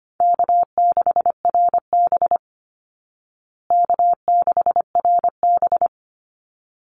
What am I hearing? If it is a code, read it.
Morse code, heard as a keyed tone, about 25 wpm, 700 Hz. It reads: K6RB K6RB
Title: Easy Call Signs Repeated 2x - 25wpm